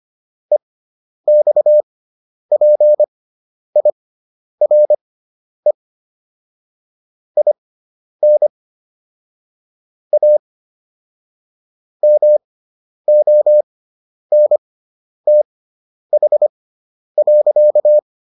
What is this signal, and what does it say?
Morse code, heard as a keyed tone, about 25 wpm, spaced as for 10 wpm, 600 Hz. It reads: EXPIRE IN A MONTH.